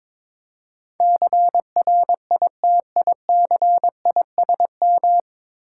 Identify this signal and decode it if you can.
Morse code, heard as a keyed tone, about 22 wpm, 690 Hz. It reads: CRITICISM
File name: 9 Letter Words - First 3 Letters Given - 22wpm